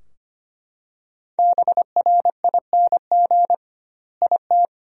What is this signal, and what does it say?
Morse code, heard as a keyed tone, about 25 wpm, 700 Hz. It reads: BRING IT
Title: 2x Word-Spacing - Sentences from Top 500 Words - 25wpm